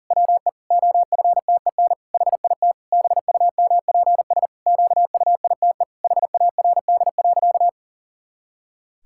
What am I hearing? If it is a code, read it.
Morse code, heard as a keyed tone, about 40 wpm, 700 Hz. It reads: WE OFTEN HIT BUMPS QUITE HARD.
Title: Pangram Challenge v2 - 40wpm